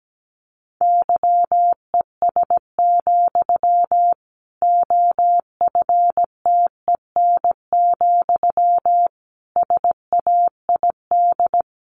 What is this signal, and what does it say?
Morse code, heard as a keyed tone, about 17 wpm, 700 Hz. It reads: YES, OFTEN, SAID
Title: Sentences from Top 1200 Words 17wpm